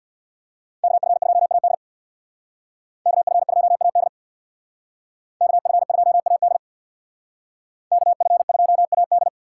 Morse code, heard as a keyed tone, about 63 wpm, 700 Hz. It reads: KF2AD KF2AD KF2AD KF2AD